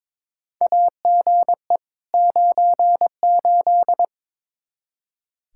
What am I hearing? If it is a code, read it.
Morse code, heard as a keyed tone, about 21 wpm, 700 Hz. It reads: AGE 98
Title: Practice QSO - Exchange 3 - More Info - 22wpm